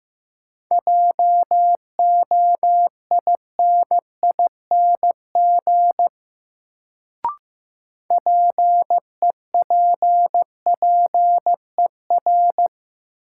Morse code, heard as a keyed tone, about 15 wpm, 700 Hz. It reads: JOINING PEPPER
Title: Sets of 5 Words - New Word Drill - 15wpm